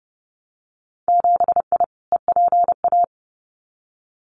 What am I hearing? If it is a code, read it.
Morse code, heard as a keyed tone, about 30 wpm, 700 Hz. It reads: 7I EPA
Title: Winter Field Day Exchanges Encore 30wpm